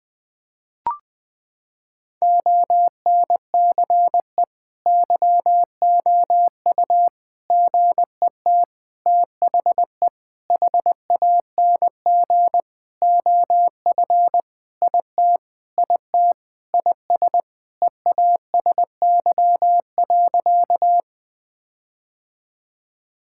Morse code, heard as a keyed tone, about 20 wpm, 700 Hz. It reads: ONCE YOU GET THE HANG OF IT IT IS EASY.
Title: Sentences with 7-Word Phrases - 20wpm